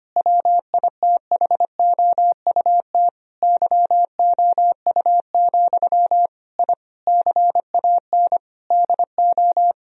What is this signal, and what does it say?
Morse code, heard as a keyed tone, about 25 wpm, 700 Hz. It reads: WITHOUT YOU, I CAN DO
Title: Sentences from Top 1000 Words - Review Part 3 - 25wpm